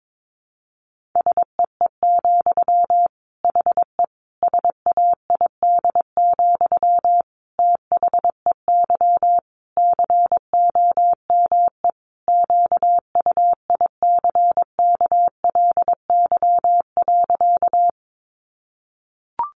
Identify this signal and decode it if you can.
Morse code, heard as a keyed tone, about 22 wpm, 700 Hz. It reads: SEE, HE SAID, THEY COME QUICKLY.
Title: Sentences from Top 700 Words 22wpm